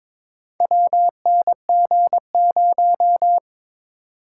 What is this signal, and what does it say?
Morse code, heard as a keyed tone, about 22 wpm, 700 Hz. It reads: WNG0
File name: practice / 22wpm / Four Letter-Number Combinations - 22wpm